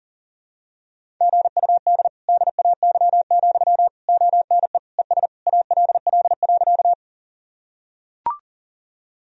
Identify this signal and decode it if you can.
Morse code, heard as a keyed tone, about 37 wpm, 700 Hz. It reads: GUD DAY, ONE ES ALL.
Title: Sentences from Top 200 Words and Abbreviations 40wpm